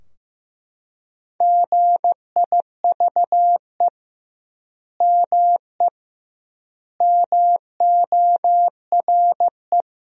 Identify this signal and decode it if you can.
Morse code, heard as a keyed tone, about 15 wpm, 700 Hz. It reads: GIVE ME MORE
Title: 2x Word-Spacing - Sentences from Top 200 Words - 15wpm